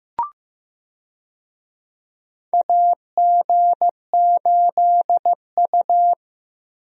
Morse code, heard as a keyed tone, about 15 wpm, 700 Hz. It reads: AG8U
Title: USA Call Signs - 4 Characters - Repeated 3x - 15wpm